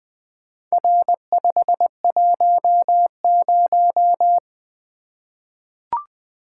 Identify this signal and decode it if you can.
Morse code, heard as a keyed tone, about 20 wpm, 700 Hz. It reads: R510
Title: Four Letter-Number Combinations - 20wpm